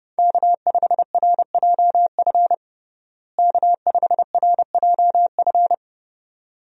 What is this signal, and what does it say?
Morse code, heard as a keyed tone, about 32 wpm, 700 Hz. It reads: K5RJF K5RJF
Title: USA Call Signs - 5 Characters - Repeated 2x - 30wpm